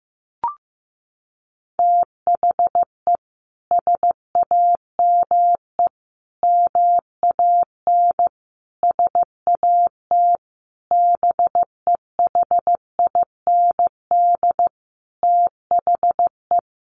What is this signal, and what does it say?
Morse code, heard as a keyed tone, about 15 wpm, 700 Hz. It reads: THE SAME MAN SAT BEHIND THE
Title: A Sound Of Thunder 15wpm